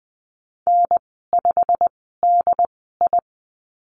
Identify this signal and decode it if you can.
Morse code, heard as a keyed tone, about 20 wpm, spaced as for 14 wpm, 700 Hz. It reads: N5DI
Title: 207 Lesson 33g - Letter j - Call Signs 20wpm